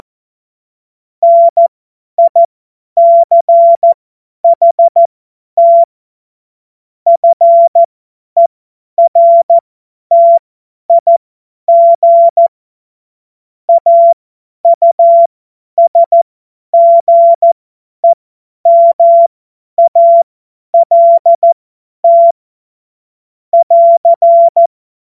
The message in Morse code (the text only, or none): NICHT FERTIG AUSGEMALT <AR>